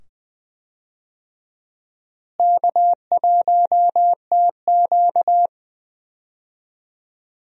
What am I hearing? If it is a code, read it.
Morse code, heard as a keyed tone, about 21 wpm, 700 Hz. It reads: K1TQ
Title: USA Call Signs - 4 Characters - Repeated 2x - 20wpm